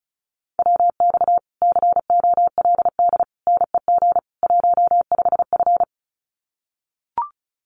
Morse code, heard as a keyed tone, about 34 wpm, 700 Hz. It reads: WX COLD NEG 15F
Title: Common QSO Phrases Encore 35wpm